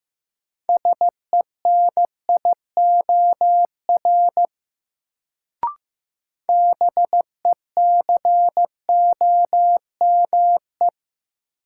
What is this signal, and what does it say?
Morse code, heard as a keyed tone, about 15 wpm, 700 Hz. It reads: SENIOR BECOME
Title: Sets of 5 Words Encore - New Word Drill - 15wpm